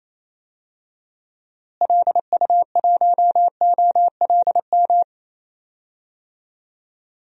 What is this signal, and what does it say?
Morse code, heard as a keyed tone, about 28 wpm, 700 Hz. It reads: LU1OLM
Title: DXCC-100 Call Signs - 6 Characters - Repeated 3x - 28wpm